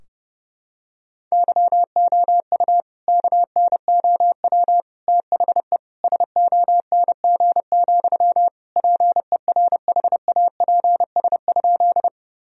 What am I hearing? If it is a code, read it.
Morse code, heard as a keyed tone, about 30 wpm, 700 Hz. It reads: YOU KNOW THE SONG, PERHAPS?
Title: Sentences from Top 1000 Words - Review Part 2 - 30wpm